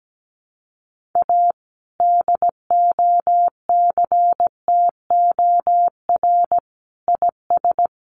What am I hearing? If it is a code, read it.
Morse code, heard as a keyed tone, about 17 wpm, 700 Hz. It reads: A DOCTOR IS
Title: Sentences from Top 1500 Words 17wpm